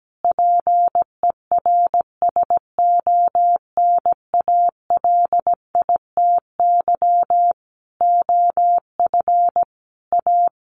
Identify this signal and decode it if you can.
Morse code, heard as a keyed tone, about 17 wpm, 700 Hz. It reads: PERSONALITY OF A
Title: Sentences from Top 1400 Words 17wpm